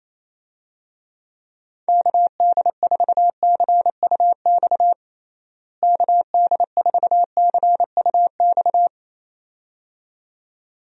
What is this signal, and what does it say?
Morse code, heard as a keyed tone, about 28 wpm, 700 Hz. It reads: KD4CUX KD4CUX
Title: USA Call Signs - 6 Characters - Repeated 2x - 28wpm